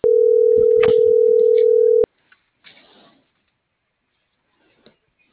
Ringback tone (North America), 440 + 480 Hz.